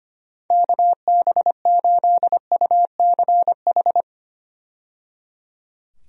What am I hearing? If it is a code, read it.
Morse code, heard as a keyed tone, about 25 wpm, 700 Hz. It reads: KB8UCH
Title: USA Call Signs - 6 Characters - 25wpm